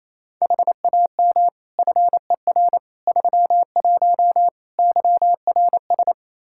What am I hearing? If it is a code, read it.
Morse code, heard as a keyed tone, about 28 wpm, 700 Hz. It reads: HAM FER 31 YRS
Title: Practice QSO - Exchange 3 - More Info - 28wpm